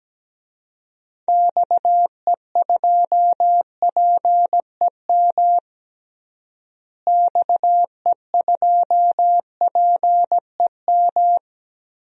Morse code, heard as a keyed tone, about 17 wpm, 700 Hz. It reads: XE2PEM XE2PEM
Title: DXCC-100 Call Signs - 6 Characters - Repeated 3x - 17wpm